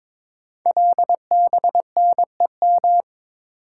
Morse code, heard as a keyed tone, about 22 wpm, 700 Hz. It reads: LBNEM
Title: Code Groups - 5 Letters - 22wpm